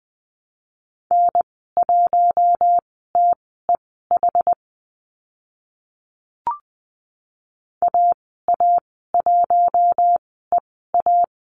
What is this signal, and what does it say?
Morse code, heard as a keyed tone, about 20 wpm, spaced as for 15 wpm, 700 Hz. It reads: N1TEH AA1EA
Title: 069 Lesson 12g - Letter d - Call Signs 20wpm